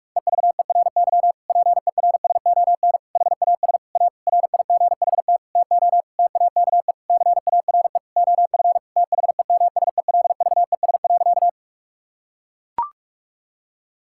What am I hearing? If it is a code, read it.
Morse code, heard as a keyed tone, about 41 wpm, 700 Hz. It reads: EVERY PERSON HAS A RIGHT TO TAKE CARE OF THEMSELVES.
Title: Sentences from Top 1000 Words - Review Part 5 - 45wpm